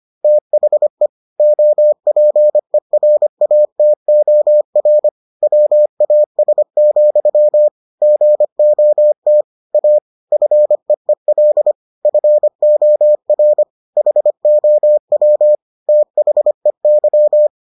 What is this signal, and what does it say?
Morse code, heard as a keyed tone, about 25 wpm, 600 Hz. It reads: THE OPERATOR WAS, GOT A FEEL FOR HOW THEY